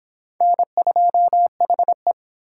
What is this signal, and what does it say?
Morse code, heard as a keyed tone, about 25 wpm, 700 Hz. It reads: N2HE